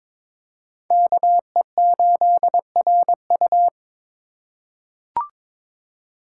Morse code, heard as a keyed tone, about 21 wpm, 700 Hz. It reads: KE8RU
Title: K1USN SST QSO Practice - 22wpm